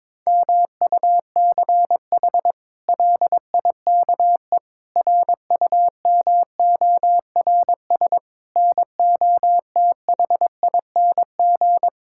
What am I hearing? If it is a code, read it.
Morse code, heard as a keyed tone, about 21 wpm, 700 Hz. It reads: MUCH LIKE RUMORS NOTHING
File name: practice / 22wpm / Sentences with 8-Word Phrases - 22wpm